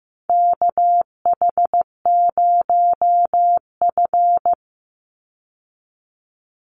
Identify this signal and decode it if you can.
Morse code, heard as a keyed tone, about 15 wpm, 700 Hz. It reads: KH0F
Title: USA Call Signs 15wpm